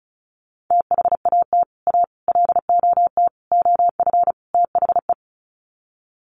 Morse code, heard as a keyed tone, about 34 wpm, 700 Hz. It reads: THAT A LOT OF THE
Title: Sets of 5 Words Encore2 35wpm